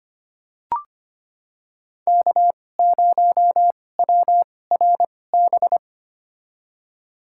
Morse code, heard as a keyed tone, about 25 wpm, spaced as for 18 wpm, 700 Hz. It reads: K0WRB
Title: USA Call Signs - 5 Characters - 2x Character Spacing - 25wpm